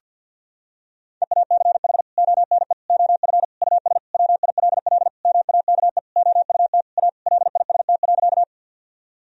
Morse code, heard as a keyed tone, about 48 wpm, 700 Hz. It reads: EACH ONE OF US WILL MAKE OUT A LIST.
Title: Sentences from Top 1000 Words - Review Part 2 - 50wpm